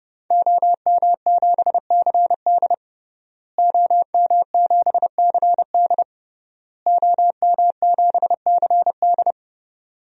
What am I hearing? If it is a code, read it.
Morse code, heard as a keyed tone, about 30 wpm, 700 Hz. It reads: OM7CD OM7CD OM7CD
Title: DXCC-100 Call Signs - 5 Characters - Repeated 3x - 30wpm